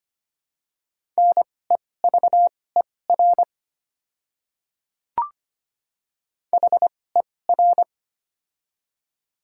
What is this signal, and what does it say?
Morse code, heard as a keyed tone, about 25 wpm, spaced as for 18 wpm, 700 Hz. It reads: NEVER HER
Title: 2x Character Spacing - Top 300 Words - 25wpm